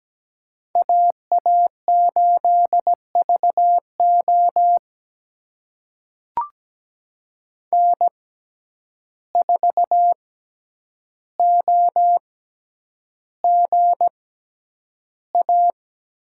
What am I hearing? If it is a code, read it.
Morse code, heard as a keyed tone, about 17 wpm, spaced as for 9 wpm, 700 Hz. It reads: AA8VO N4OGA
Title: USA Call Signs - 5 Characters - 6x Character Spacing - 17wpm